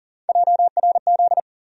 Morse code, heard as a keyed tone, about 40 wpm, 700 Hz. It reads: JRZ
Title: USA Call Sign Postfixes - 3 Letters 40wpm